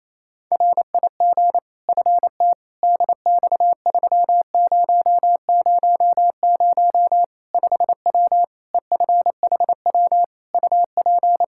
Morse code, heard as a keyed tone, about 28 wpm, 700 Hz. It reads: RIG FT DX3000 5W EFHW UP
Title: K1USN SST QSO Practice 28wpm